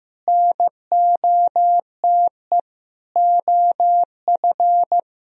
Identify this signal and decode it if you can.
Morse code, heard as a keyed tone, about 15 wpm, 700 Hz. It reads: NOTE OF
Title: Sentences from Top 500 Words and Abbreviations 15wpm